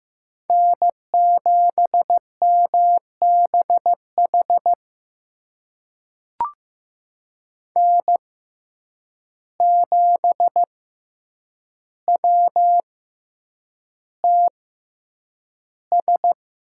Morse code, heard as a keyed tone, about 15 wpm, spaced as for 8 wpm, 700 Hz. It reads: N7MBH N7WTS
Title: USA Call Signs - 5 Characters - 6x Character Spacing - 15wpm